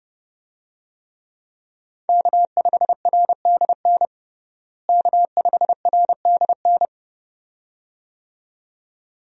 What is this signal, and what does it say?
Morse code, heard as a keyed tone, about 30 wpm, 700 Hz. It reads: K5RDN K5RDN
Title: USA Call Signs - 5 Characters - Repeated 2x - 30wpm